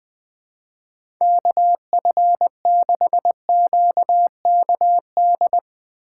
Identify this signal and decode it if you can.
Morse code, heard as a keyed tone, about 21 wpm, 700 Hz. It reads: KF6QKD
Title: USA Call Signs - 6 Characters - 2x Character Spacing - 20wpm